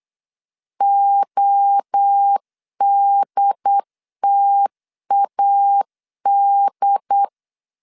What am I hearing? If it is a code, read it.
Morse code, heard as a keyed tone, about 8 wpm, 800 Hz. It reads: ODTAD
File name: CW / DET / 1 3 1500C S B 8wpm 4141